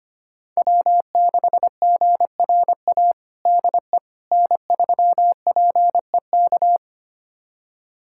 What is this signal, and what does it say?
Morse code, heard as a keyed tone, about 25 wpm, 700 Hz. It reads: W6GRA DE N3PEK
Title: USA Callsign DE Callsign 25wpm